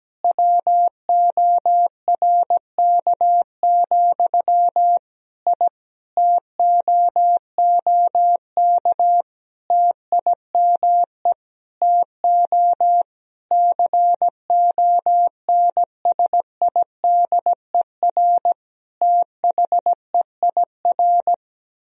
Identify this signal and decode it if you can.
Morse code, heard as a keyed tone, about 17 wpm, 700 Hz. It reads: WORK, I TOOK TIME TO CONSIDER THEIR
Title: Sentences from Top 1000 Words - Review Part 4 - 17wpm